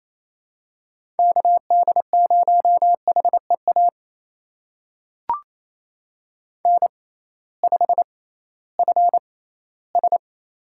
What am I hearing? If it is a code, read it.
Morse code, heard as a keyed tone, about 27 wpm, spaced as for 16 wpm, 700 Hz. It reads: KD0HEA N5FS
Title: USA Call Signs - 6x Character Spacing - 28wpm